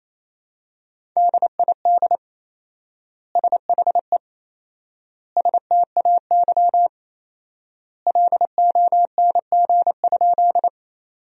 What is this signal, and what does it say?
Morse code, heard as a keyed tone, about 28 wpm, 700 Hz. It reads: DID SHE STAY LONG?
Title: 4x Word-Spacing - Sentences from Top 500 Words - 28wpm